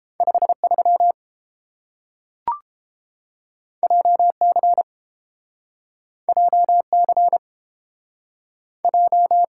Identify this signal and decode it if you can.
Morse code, heard as a keyed tone, about 30 wpm, 700 Hz. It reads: 53 JC JC J